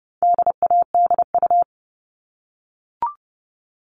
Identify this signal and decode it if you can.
Morse code, heard as a keyed tone, about 30 wpm, 700 Hz. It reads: DADU
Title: All Cut Numbers - 4-Digit 30wpm